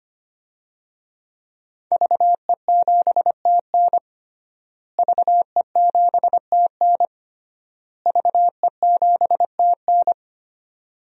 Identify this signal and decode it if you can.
Morse code, heard as a keyed tone, about 25 wpm, 700 Hz. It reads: VE7TN VE7TN VE7TN
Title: DXCC-100 Call Signs - 5 Characters - Repeated 3x - 25wpm